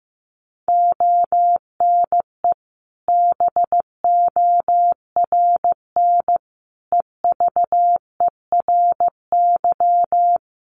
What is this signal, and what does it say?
Morse code, heard as a keyed tone, about 15 wpm, 700 Hz. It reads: ONE BORN EVERY
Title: Sentences from Top 1400 Words 15wpm